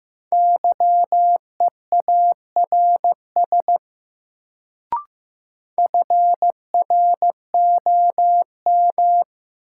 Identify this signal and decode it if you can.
Morse code, heard as a keyed tone, about 15 wpm, 700 Hz. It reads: YEARS FROM
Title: Sets of 2 Words - New Word Drill - 15wpm